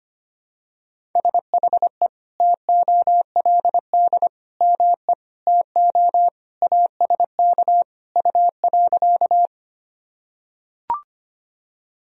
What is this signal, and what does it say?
Morse code, heard as a keyed tone, about 25 wpm, 700 Hz. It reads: SHE TOLD ME TO ASK U.
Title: Sentences from Top 400 Words and Abbreviations 25wpm